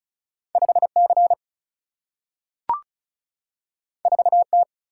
Morse code, heard as a keyed tone, about 36 wpm, 700 Hz. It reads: 5C 4T
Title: Single-Letter Single-Number 35wpm